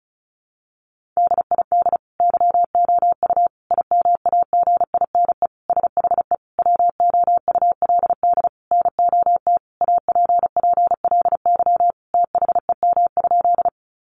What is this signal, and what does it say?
Morse code, heard as a keyed tone, about 34 wpm, 700 Hz. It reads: DID YOU IMAGINE SHE WOULD NOT APPLY THEM?
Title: Sentences from Top 1400 Words 35wpm